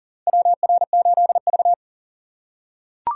Morse code, heard as a keyed tone, about 40 wpm, 700 Hz. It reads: WR8V
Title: USA Call Signs - 4 Characters - 2x Character Spacing - 40wpm